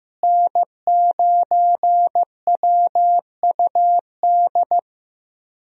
Morse code, heard as a keyed tone, about 15 wpm, 700 Hz. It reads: N9WUD
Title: USA Call Signs - 5 Characters - Repeated 3x - 15wpm